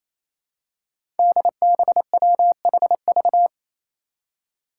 Morse code, heard as a keyed tone, about 28 wpm, 700 Hz. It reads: DBWHV
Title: Five Challenging Letter-Number Combinations 28wpm